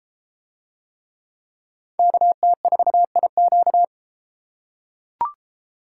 Morse code, heard as a keyed tone, about 33 wpm, 700 Hz. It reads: KT4IQ